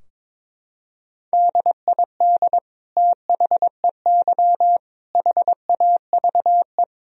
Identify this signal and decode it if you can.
Morse code, heard as a keyed tone, about 22 wpm, 700 Hz. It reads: DID THEY HAVE
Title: Sentences from Top 1000 Words - Review Part 3 - 22wpm